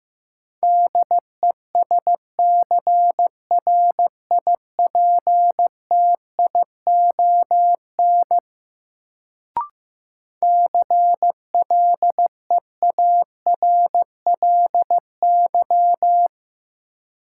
Morse code, heard as a keyed tone, about 15 wpm, 700 Hz. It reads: DESCRIPTION CLEARLY
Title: Sentences from Top 1300 Words - New Word Drill Alt - 15wpm